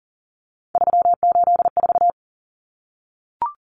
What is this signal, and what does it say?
Morse code, heard as a keyed tone, about 40 wpm, 700 Hz. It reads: 384
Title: Numbers - 3-Digit 40wpm